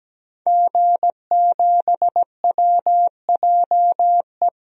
Morse code, heard as a keyed tone, about 17 wpm, 700 Hz. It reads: G7WJE
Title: DXCC-100 Call Signs - 5 Characters - 17wpm